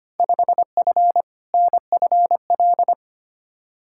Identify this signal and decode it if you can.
Morse code, heard as a keyed tone, about 25 wpm, 700 Hz. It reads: 5F NFL
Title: Field Day Exchanges Encore - 25wpm